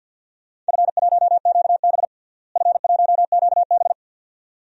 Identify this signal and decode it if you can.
Morse code, heard as a keyed tone, about 52 wpm, 700 Hz. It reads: F1QB F1QB
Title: DXCC-100 Call Signs - 4 Characters - Repeated 2x - 50wpm